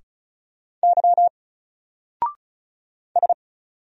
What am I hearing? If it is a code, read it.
Morse code, heard as a keyed tone, about 35 wpm, 700 Hz. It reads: Y S
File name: Single-Letter-Number-35wpm